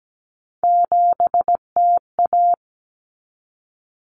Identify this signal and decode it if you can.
Morse code, heard as a keyed tone, about 17 wpm, 690 Hz. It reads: 7TA
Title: Common Cut Numbers - 3-Digit 17wpm